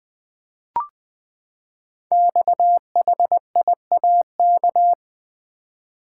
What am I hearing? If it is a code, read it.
Morse code, heard as a keyed tone, about 20 wpm, 700 Hz. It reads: XHIAK
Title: Code Groups - 5 Letter:Numbers - 20wpm